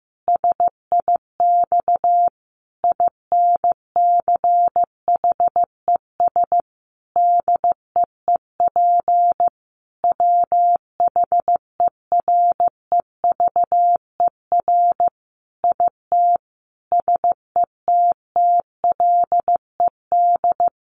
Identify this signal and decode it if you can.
Morse code, heard as a keyed tone, about 15 wpm, 700 Hz. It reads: SIX INCHES DEEP WHEREVER IT SETTLED
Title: A Sound Of Thunder 15wpm